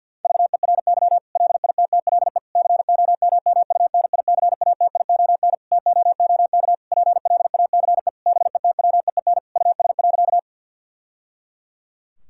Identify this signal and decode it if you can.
Morse code, heard as a keyed tone, about 50 wpm, 700 Hz. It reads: VERY LITTLE COMMUNICATION TOOK PLACE BETWEEN US.
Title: Sentences from Top 1000 Words - Review Part 6 - 50wpm